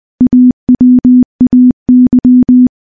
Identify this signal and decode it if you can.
Morse code, heard as a keyed tone, about 20 wpm, 250 Hz. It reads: AWAY
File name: morse / morse-training-20WPM